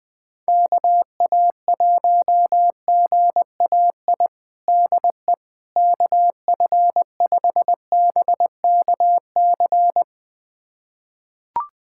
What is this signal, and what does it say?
Morse code, heard as a keyed tone, about 20 wpm, 700 Hz. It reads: KA1GAI DE KF5BKC
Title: USA Callsign DE Callsign - 6 Characters - 20wpm